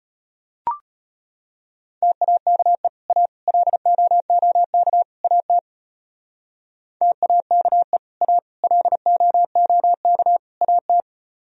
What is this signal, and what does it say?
Morse code, heard as a keyed tone, about 32 wpm, 700 Hz. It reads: TAKE A LOOK AT TAKE A LOOK AT